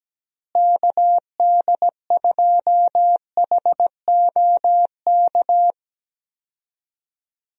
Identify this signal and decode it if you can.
Morse code, heard as a keyed tone, about 17 wpm, 700 Hz. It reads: KD2HOK
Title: USA Call Signs - 6 Characters - Repeated 2x - 17wpm